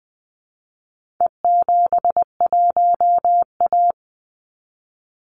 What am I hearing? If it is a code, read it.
Morse code, heard as a keyed tone, about 20 wpm, 700 Hz. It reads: E71A
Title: Call Signs - 20wpm